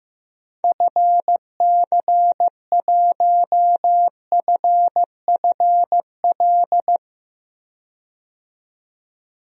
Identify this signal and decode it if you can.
Morse code, heard as a keyed tone, about 16 wpm, 700 Hz. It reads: FC1FFL
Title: DXCC-100 Call Signs - 6 Characters - 15wpm